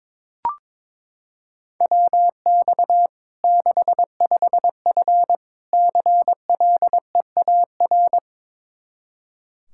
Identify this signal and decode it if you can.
Morse code, heard as a keyed tone, about 22 wpm, 700 Hz. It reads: WX 65F CLEAR
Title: Practice QSO - Exchange 2 - Info - 22wpm